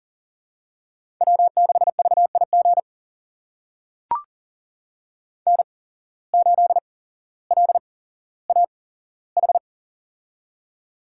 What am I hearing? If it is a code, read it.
Morse code, heard as a keyed tone, about 40 wpm, spaced as for 17 wpm, 700 Hz. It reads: W6VIG N8LAH
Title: USA Call Signs - 5 Characters - 8x Character Spacing - 40wpm